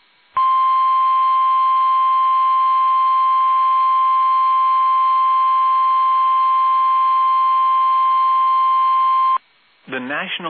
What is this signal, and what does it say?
Emergency Alert System (SAME) tones (the 1050 Hz attention tone), with speech. It is heard from 0.3 to 9.4 s.